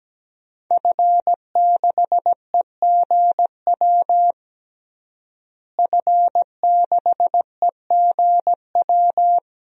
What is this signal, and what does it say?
Morse code, heard as a keyed tone, about 17 wpm, 700 Hz. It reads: F6EGW F6EGW
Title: DXCC-100 Call Signs - 5 Characters - Repeated 3x - 17wpm